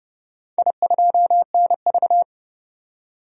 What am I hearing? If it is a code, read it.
Morse code, heard as a keyed tone, about 30 wpm, 700 Hz. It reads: I2NV
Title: DXCC-100 Call Signs - 4 Characters - 30wpm